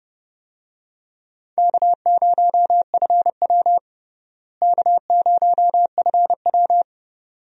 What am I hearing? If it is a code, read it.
Morse code, heard as a keyed tone, about 32 wpm, 700 Hz. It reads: K0FW K0FW
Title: USA Call Signs - 4 Characters - Repeated 2x - 30wpm